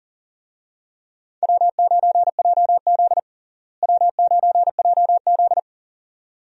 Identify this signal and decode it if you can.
Morse code, heard as a keyed tone, about 40 wpm, 700 Hz. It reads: W9JZ W9JZ
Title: USA Call Signs - 4 Characters - Repeated 2x - 40wpm